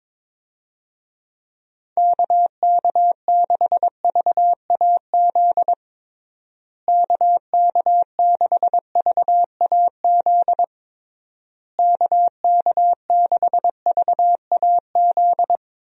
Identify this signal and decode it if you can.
Morse code, heard as a keyed tone, about 22 wpm, 700 Hz. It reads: KK6VAZ KK6VAZ KK6VAZ
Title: USA Call Signs - 6 Characters - Repeated 3x - 22wpm